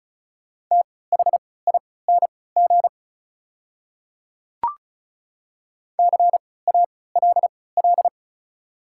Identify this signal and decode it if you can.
Morse code, heard as a keyed tone, about 36 wpm, spaced as for 20 wpm, 700 Hz. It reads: THING CALL
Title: 3x Character Spacing - Top 100 Words - 35wpm